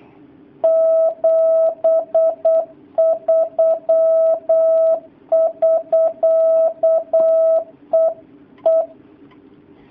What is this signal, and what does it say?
Morse code, heard as a keyed tone, about 8 wpm, 650 Hz. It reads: 73 <SK> EE
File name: jh8cbh / 2019cw / sp08- w